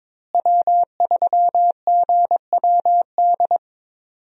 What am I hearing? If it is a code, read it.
Morse code, heard as a keyed tone, about 22 wpm, 700 Hz. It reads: W3GWD
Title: USA Call Signs - 5 Characters - Repeated 2x - 22wpm